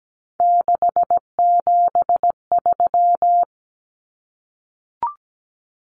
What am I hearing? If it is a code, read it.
Morse code, heard as a keyed tone, about 17 wpm, 700 Hz. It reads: 673
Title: Common Cut Numbers - 3-Digit 17wpm